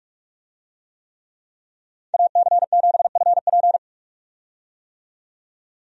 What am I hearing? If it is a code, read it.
Morse code, heard as a keyed tone, about 43 wpm, 700 Hz. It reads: AC7FP
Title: USA Call Signs - 5 Characters - 45wpm